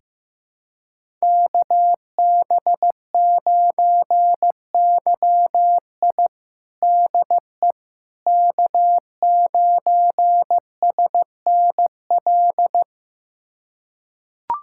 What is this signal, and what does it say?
Morse code, heard as a keyed tone, about 16 wpm, 700 Hz. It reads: KB9YI DE K9SNL
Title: USA Callsign DE Callsign - 5 Characters - 15wpm